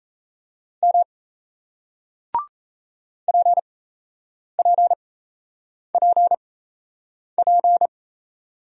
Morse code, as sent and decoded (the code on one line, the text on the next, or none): -- / .--. .--. .--. .--.
M PPPP